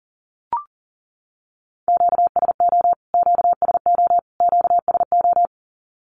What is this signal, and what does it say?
Morse code, heard as a keyed tone, about 40 wpm, 700 Hz. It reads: QSO QSO QSO
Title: Q-Codes Repeated 3x 40wpm